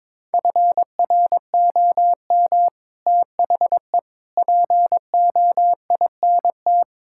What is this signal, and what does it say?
Morse code, heard as a keyed tone, about 21 wpm, 700 Hz. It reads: FROM THE POINT
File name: practice / 22wpm / Sets of 6 Words - 22wpm